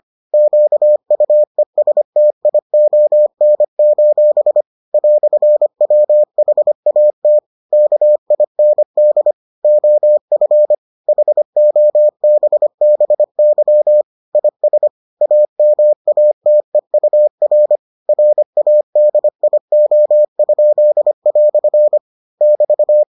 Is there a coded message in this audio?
Yes: Morse code from 0.3 to 23.0 s, heard as a keyed tone, about 25 wpm, 600 Hz. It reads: QUESTION: "WHAT KIND OF HOBBY IS AMATEUR RADIO?" <BT>